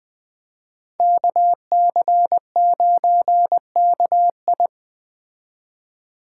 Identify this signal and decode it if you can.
Morse code, heard as a keyed tone, about 21 wpm, 700 Hz. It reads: KC9KI